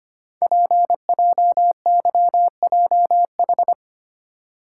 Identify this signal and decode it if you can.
Morse code, heard as a keyed tone, about 25 wpm, 700 Hz. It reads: PJYJH
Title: Code Groups - 5 Letters - 25wpm